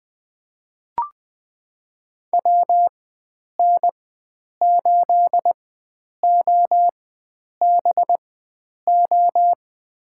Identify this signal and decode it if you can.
Morse code, heard as a keyed tone, about 21 wpm, spaced as for 9 wpm, 700 Hz. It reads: WN8OBO
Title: USA Call Signs - 4x Character Spacing - 20wpm